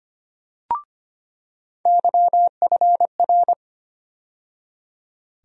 Morse code, heard as a keyed tone, about 25 wpm, 700 Hz. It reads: YFR
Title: USA Call Sign Postfixes - 3 Letters 25wpm